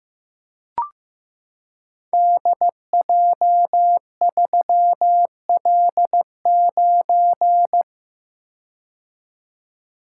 Morse code, heard as a keyed tone, about 15 wpm, 700 Hz. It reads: DJ3L9
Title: Code Groups - 5 Letter:Numbers - 15wpm